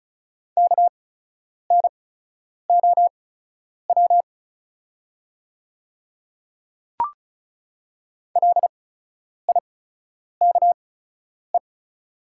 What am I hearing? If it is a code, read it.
Morse code, heard as a keyed tone, about 36 wpm, spaced as for 10 wpm, 700 Hz. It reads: KNOW LIKE
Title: 8x Character Spacing - Top 100 Words - 35wpm